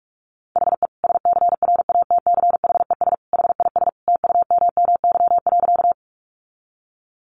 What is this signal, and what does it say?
Morse code, heard as a keyed tone, about 44 wpm, 700 Hz. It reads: HE SCRATCHES HIS TUMMY.
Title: Easy Phrases 45wpm